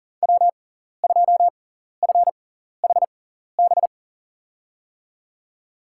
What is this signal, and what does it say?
Morse code, heard as a keyed tone, about 40 wpm, spaced as for 14 wpm, 700 Hz. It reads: W2FHB
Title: USA Call Signs - 5 Characters - 6x Character Spacing - 40wpm